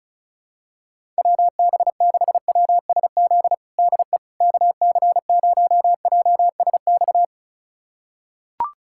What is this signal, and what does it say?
Morse code, heard as a keyed tone, about 36 wpm, 700 Hz. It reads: WB6WSZ DE KC0JSX
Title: USA Callsign DE Callsign 35wpm